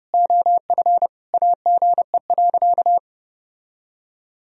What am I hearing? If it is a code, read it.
Morse code, heard as a keyed tone, about 32 wpm, 700 Hz. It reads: OF AGE.
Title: Sentences from Top 1000 Words - Review Part 3 - 30wpm